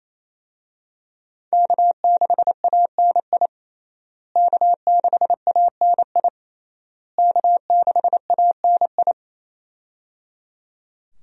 Morse code, heard as a keyed tone, about 28 wpm, 700 Hz. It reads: K6ANI K6ANI K6ANI
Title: USA Call Signs - 5 Characters - Repeated 3x - 28wpm